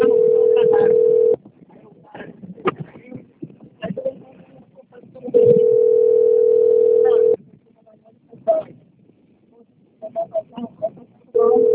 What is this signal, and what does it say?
Ringback tone (North America), 440 + 480 Hz, with speech.
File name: out- - - - - .59848